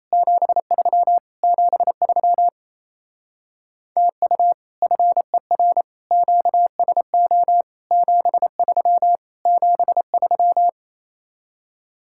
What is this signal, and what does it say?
Morse code, heard as a keyed tone, about 35 wpm, 700 Hz. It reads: 73 73 TU FER QSO 73 73